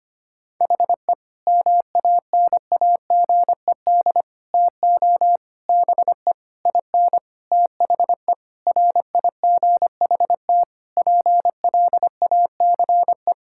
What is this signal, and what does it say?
Morse code, heard as a keyed tone, about 25 wpm, 700 Hz. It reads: HE MANAGED TO BE IN THE RIGHT PLACE
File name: practice / 25wpm / Sentences with 9-Word Phrases - 25wpm